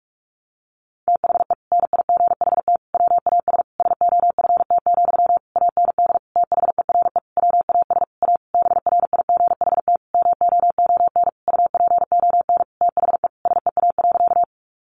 Morse code, heard as a keyed tone, about 44 wpm, 700 Hz. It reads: THE NIGHT WAS SOFT, AND THERE WAS A BRIGHT MOON UPON THE SEA.
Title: Sentences from Top 1500 Words 45wpm